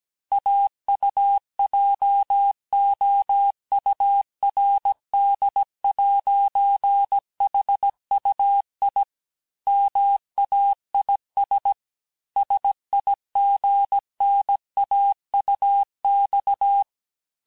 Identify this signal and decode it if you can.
Morse code, heard as a keyed tone, about 17 wpm, 800 Hz. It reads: AUJOURD'HUI MAIS SIGNAUX
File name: cours-du-lundi files / cours-1587-FIN-de-QSO-vitesse-16MPM